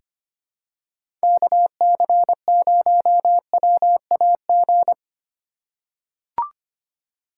Morse code, heard as a keyed tone, about 25 wpm, 700 Hz. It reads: KC0WAG
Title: USA Call Signs - 6 Characters - 3x Character Spacing - 25wpm